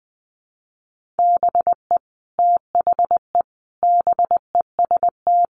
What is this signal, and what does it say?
Morse code, heard as a keyed tone, about 20 wpm, 700 Hz. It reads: BE THE BEST
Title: Sentences from Top 1400 Words 20wpm